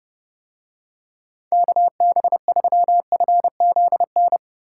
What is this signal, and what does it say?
Morse code, heard as a keyed tone, about 30 wpm, 700 Hz. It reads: KB3FZN